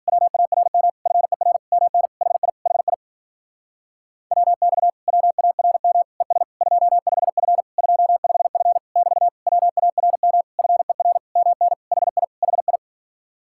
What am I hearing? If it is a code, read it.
Morse code, heard as a keyed tone, about 52 wpm, 700 Hz. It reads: WARM FER MN HI HI WX WARM ES 25F 25F <BT> WARM FER MN HI HI